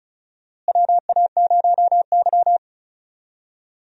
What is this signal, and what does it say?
Morse code, heard as a keyed tone, about 34 wpm, 700 Hz. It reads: WA0Y
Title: USA Call Signs - 4 Characters - Repeated 3x - 35wpm